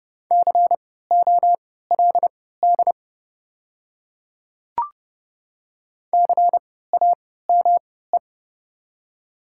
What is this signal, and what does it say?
Morse code, heard as a keyed tone, about 30 wpm, spaced as for 17 wpm, 690 Hz. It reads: COLD CAME
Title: 3x Character Spacing - Top 500 Words - 30wpm